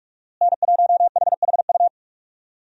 Morse code, heard as a keyed tone, about 46 wpm, 700 Hz. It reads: N1HHU
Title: USA Call Signs - 5 Characters - Repeated 3x - 45wpm